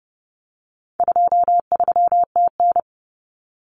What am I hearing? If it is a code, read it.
Morse code, heard as a keyed tone, about 30 wpm, 700 Hz. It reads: 23TN
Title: Common Cut Numbers - 4-Digit 30wpm